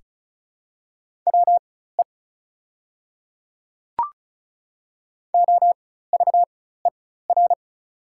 Morse code, heard as a keyed tone, about 36 wpm, spaced as for 16 wpm, 700 Hz. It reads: WE OVER
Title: 4x Character Spacing - Top 200 Words - 35wpm